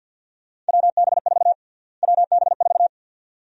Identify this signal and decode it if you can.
Morse code, heard as a keyed tone, about 50 wpm, 700 Hz. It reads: WB4 WB4
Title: USA Call Sign Prefixes - Repeated 2x - 50wpm